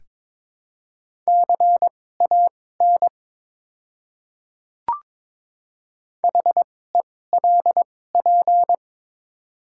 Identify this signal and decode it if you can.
Morse code, heard as a keyed tone, about 22 wpm, spaced as for 16 wpm, 700 Hz. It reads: CAN HELP
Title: 2x Character Spacing - Top 200 Words - 22wpm